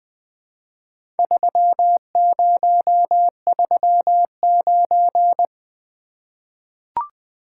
Morse code, heard as a keyed tone, about 20 wpm, 700 Hz. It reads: 3039
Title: Four Letter-Number Combinations - 20wpm